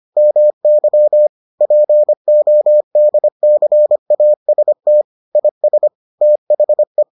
Morse code, heard as a keyed tone, about 25 wpm, 600 Hz. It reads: MY PODCAST IS THE